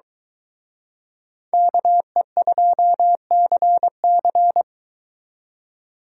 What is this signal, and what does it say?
Morse code, heard as a keyed tone, about 23 wpm, 700 Hz. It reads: KE2CC